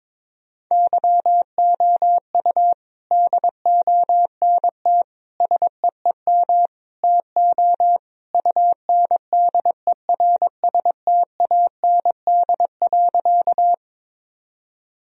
Morse code, heard as a keyed tone, about 22 wpm, 700 Hz. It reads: YOU DONT SEEM TO UNDERSTAND.
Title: Sentences from Top 1000 Words - Review Part 1 - 22wpm